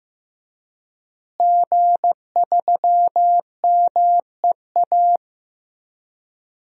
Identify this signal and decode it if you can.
Morse code, heard as a keyed tone, about 15 wpm, 700 Hz. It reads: G3MEA
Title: DXCC-100 Call Signs - 5 Characters - Repeated 2x - 15wpm